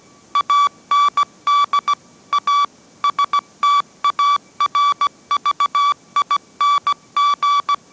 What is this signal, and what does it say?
Morse code, heard as a keyed tone, about 17 wpm, 1150 Hz. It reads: AND A STARVING
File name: mrx / mrx080 18wpm